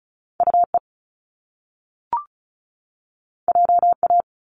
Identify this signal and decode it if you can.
Morse code, heard as a keyed tone, about 36 wpm, 700 Hz. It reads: UE JA
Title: Top Two Letter Combinations in Words 35wpm